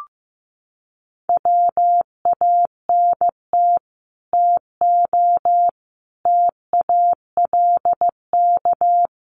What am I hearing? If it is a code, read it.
Morse code, heard as a keyed tone, about 15 wpm, 700 Hz. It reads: WANT TO TALK
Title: Sets of 5 Words Encore 15wpm